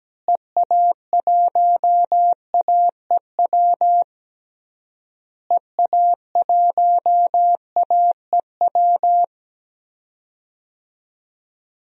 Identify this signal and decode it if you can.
Morse code, heard as a keyed tone, about 17 wpm, 700 Hz. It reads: EA1AEW EA1AEW
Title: DXCC-100 Call Signs - 6 Characters - Repeated 3x - 17wpm